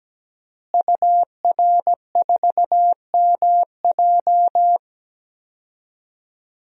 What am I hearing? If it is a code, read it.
Morse code, heard as a keyed tone, about 17 wpm, 700 Hz. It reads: UR4MJ
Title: DXCC-100 Call Signs - 5 Characters - Repeated 2x - 17wpm